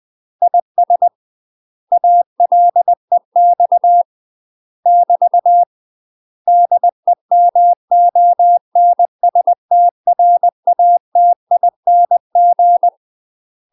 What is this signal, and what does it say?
Morse code, heard as a keyed tone, about 20 wpm, 700 Hz. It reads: IS ALEX <BT> DEMONSTRATING
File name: UK MASTER QSOs TRAINING / 20WPM / UK MASTER QSOs 0064